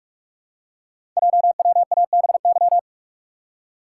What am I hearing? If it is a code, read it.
Morse code, heard as a keyed tone, about 44 wpm, 690 Hz. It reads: JWABY
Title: Code Groups - 5 Letters - 45wpm